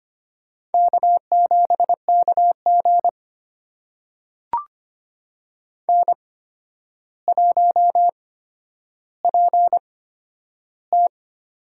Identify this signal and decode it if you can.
Morse code, heard as a keyed tone, about 25 wpm, spaced as for 11 wpm, 700 Hz. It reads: K7KG N1PT
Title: USA Call Signs - 4 Characters - 8x Character Spacing - 25wpm